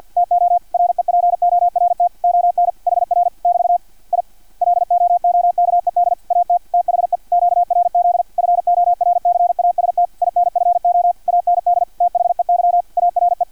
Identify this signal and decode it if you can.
Morse code, heard as a keyed tone, about 50 wpm, 700 Hz, with radio noise. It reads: TO REPORT ON HR <BT> I LOOKED AT THE QRZ FORCAST INFO AND THEY ARE
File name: archived / CRSnet